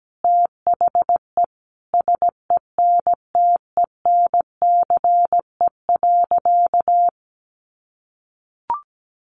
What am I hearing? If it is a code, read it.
Morse code, heard as a keyed tone, about 17 wpm, 700 Hz. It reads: THE SENTENCE.
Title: Sentences from Top 1200 Words 17wpm